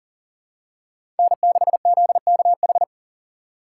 Morse code, heard as a keyed tone, about 38 wpm, 700 Hz. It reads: N6ZKH
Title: USA Call Signs - 5 Characters - 3x Character Spacing - 40wpm